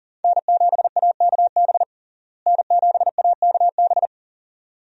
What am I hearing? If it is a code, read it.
Morse code, heard as a keyed tone, about 40 wpm, 700 Hz. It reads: N7AKB N7AKB
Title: USA Call Signs - 5 Characters - Repeated 2x - 40wpm